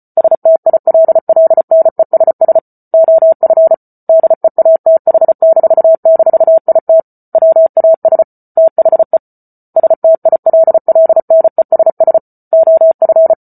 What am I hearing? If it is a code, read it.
Morse code, heard as a keyed tone, about 34 wpm, 650 Hz. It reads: STILLNESS OF DEATH--IT WAS THE STILLNESS OF